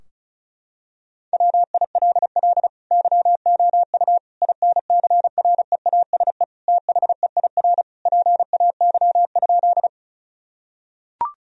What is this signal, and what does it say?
Morse code, heard as a keyed tone, about 36 wpm, 700 Hz. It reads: WILL YOU INCREASE THEIR PAY?
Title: Sentences from Top 1000 Words - Review Part 4 - 35wpm